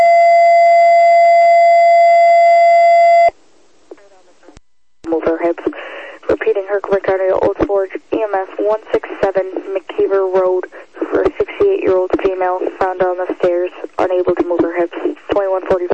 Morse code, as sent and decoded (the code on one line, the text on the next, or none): none